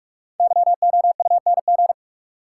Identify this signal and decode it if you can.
Morse code, heard as a keyed tone, about 46 wpm, 700 Hz. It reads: YOUNG
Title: Sentences from Top 400 Words - New Word Drill - 45wpm